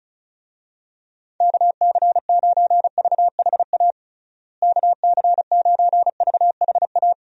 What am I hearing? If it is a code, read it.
Morse code, heard as a keyed tone, about 34 wpm, 700 Hz. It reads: KC9VHA KC9VHA
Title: USA Call Signs - 6 Characters - Repeated 3x - 35wpm